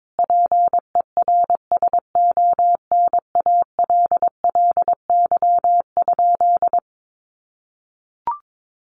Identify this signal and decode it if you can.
Morse code, heard as a keyed tone, about 22 wpm, 700 Hz. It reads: PERSONALLY?
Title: Sentences from Top 1400 Words 22wpm